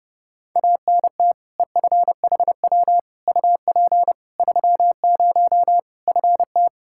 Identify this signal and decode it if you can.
Morse code, heard as a keyed tone, about 31 wpm, 700 Hz. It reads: ANT EFHW UP 30 FT
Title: Practice QSO - Exchange 2 - Info - 30wpm